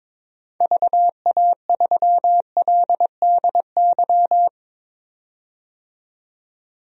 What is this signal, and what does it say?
Morse code, heard as a keyed tone, about 22 wpm, 700 Hz. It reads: VA3LDY
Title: DXCC-100 Call Signs - 6 Characters - 22wpm